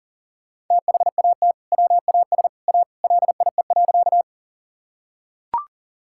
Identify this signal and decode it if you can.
Morse code, heard as a keyed tone, about 40 wpm, 700 Hz. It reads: THAT WAS A LIE.
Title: Sentences from Top 1600 Words 40wpm